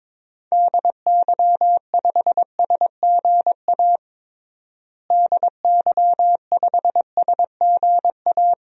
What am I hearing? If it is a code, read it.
Morse code, heard as a keyed tone, about 22 wpm, 700 Hz. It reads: DY5SGA DY5SGA
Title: DXCC-100 Call Signs - 6 Characters - Repeated 2x - 22wpm